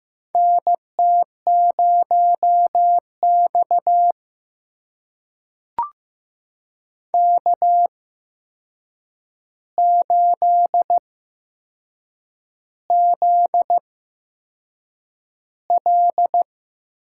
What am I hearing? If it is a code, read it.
Morse code, heard as a keyed tone, about 15 wpm, spaced as for 6 wpm, 700 Hz. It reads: NT0X K8ZL
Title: USA Call Signs - 4 Characters - 8x Character Spacing - 15wpm